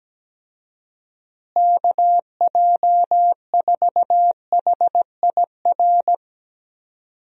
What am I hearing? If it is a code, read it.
Morse code, heard as a keyed tone, about 17 wpm, 700 Hz. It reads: KJ4HIR
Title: USA Call Signs - 6 Characters - Repeated 3x - 17wpm